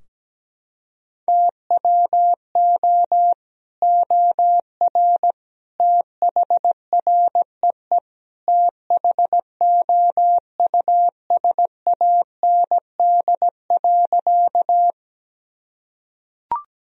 Morse code, heard as a keyed tone, about 17 wpm, 700 Hz. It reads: TWO OR THREE THOUSAND.
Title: Sentences from Top 1000 Words - Review Part 3 - 17wpm